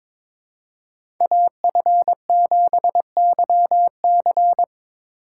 Morse code, heard as a keyed tone, about 26 wpm, 700 Hz. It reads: AF7YC
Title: Speed Racing - USA 2x2 Call Signs 22wpm